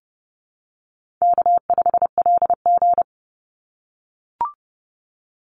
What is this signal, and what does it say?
Morse code, heard as a keyed tone, about 30 wpm, 700 Hz. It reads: K5LG
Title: Easy Call Signs - 30wpm